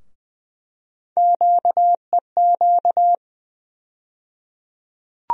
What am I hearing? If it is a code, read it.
Morse code, heard as a keyed tone, about 20 wpm, 700 Hz. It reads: QEQ
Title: USA Call Sign Postfixes - 3 Letters 20wpm